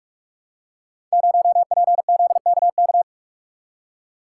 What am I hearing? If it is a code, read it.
Morse code, heard as a keyed tone, about 44 wpm, 700 Hz. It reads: 0PZKK
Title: Code Groups - 5 Letter:Numbers - 45wpm